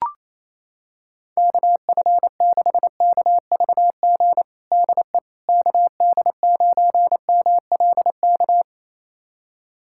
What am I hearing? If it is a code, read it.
Morse code, heard as a keyed tone, about 28 wpm, 700 Hz. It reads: KF6KVG DE KD9MLK
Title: USA Callsign DE Callsign 28wpm